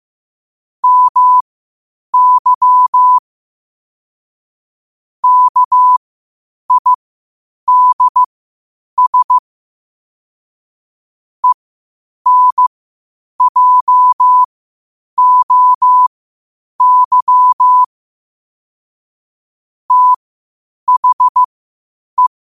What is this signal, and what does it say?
Morse code, heard as a keyed tone, about 15 wpm, spaced as for 8 wpm, 1000 Hz. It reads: MY KIDS ENJOY THE
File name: qotdfiles / 8wpm